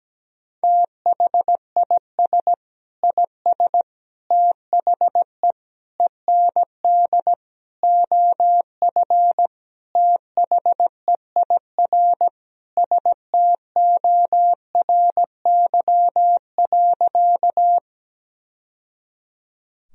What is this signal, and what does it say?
Morse code, heard as a keyed tone, about 17 wpm, 700 Hz. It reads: THIS IS THE END OF THEIR STORY.
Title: Sentences from Top 1000 Words - Review Part 1 - 17wpm